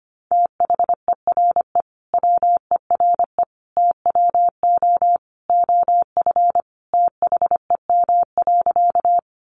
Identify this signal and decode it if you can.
Morse code, heard as a keyed tone, about 25 wpm, 700 Hz. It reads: THERE WERE TWO OF THEM.
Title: Sentences from Top 100 Words 25wpm